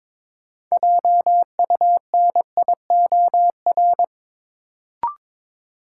Morse code, heard as a keyed tone, about 21 wpm, 700 Hz. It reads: JUNIOR
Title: Sentences from Top 1000 Words - Review Part 7 - New Word Drill - 22wpm